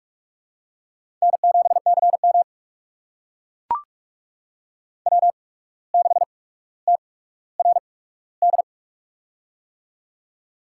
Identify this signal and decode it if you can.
Morse code, heard as a keyed tone, about 41 wpm, spaced as for 17 wpm, 700 Hz. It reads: N7CM W6TRD
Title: USA Call Signs - 8x Character Spacing - 45wpm